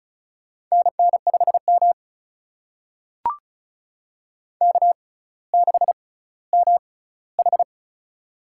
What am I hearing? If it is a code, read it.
Morse code, heard as a keyed tone, about 34 wpm, spaced as for 18 wpm, 700 Hz. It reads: NN5M K6MH
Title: USA Call Signs - 4 Characters - 6x Character Spacing - 35wpm